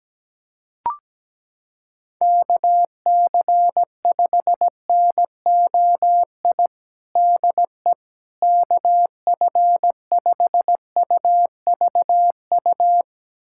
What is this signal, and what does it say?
Morse code, heard as a keyed tone, about 17 wpm, 700 Hz. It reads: KC5NOI DE KF5UVU
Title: USA Callsign DE Callsign - 6 Characters - 17wpm